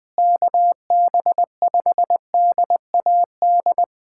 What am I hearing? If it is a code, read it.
Morse code, heard as a keyed tone, about 20 wpm, 700 Hz. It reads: KB5DAD
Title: USA Callsign DE Callsign - 6 Characters - 20wpm